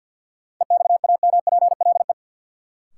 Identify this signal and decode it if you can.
Morse code, heard as a keyed tone, about 52 wpm, 700 Hz. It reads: EXAMPLE
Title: Sentences from Top 400 Words - New Word Drill - 50wpm